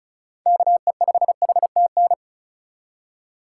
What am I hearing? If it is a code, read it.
Morse code, heard as a keyed tone, about 34 wpm, 700 Hz. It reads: KE5HTN
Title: USA Call Signs - 6 Characters - 2x Character Spacing - 35wpm